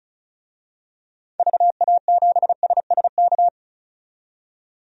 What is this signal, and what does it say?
Morse code, heard as a keyed tone, about 36 wpm, 700 Hz. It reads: VA7SSK
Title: DXCC-100 Call Signs - 6 Characters - 35wpm